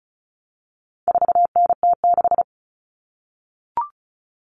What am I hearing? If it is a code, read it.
Morse code, heard as a keyed tone, about 34 wpm, 700 Hz. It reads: 4NT6
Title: All Cut Numbers - 4-Digit 35wpm